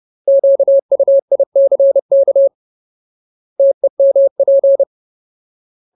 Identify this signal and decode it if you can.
Morse code, heard as a keyed tone, about 30 wpm, 550 Hz. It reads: QUICK TEMP
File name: Ultimate Word List / 30WPM Ultimate